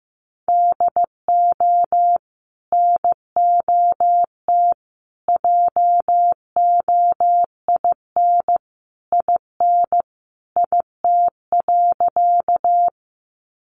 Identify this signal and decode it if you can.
Morse code, heard as a keyed tone, about 15 wpm, 700 Hz. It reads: DO NOT JOIN IN IT.
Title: Sentences from Top 1400 Words 15wpm